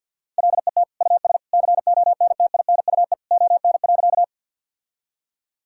Morse code, heard as a keyed tone, about 50 wpm, 700 Hz. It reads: LET US CONTINUE ON.
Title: Sentences from Top 1000 Words - Review Part 6 - 50wpm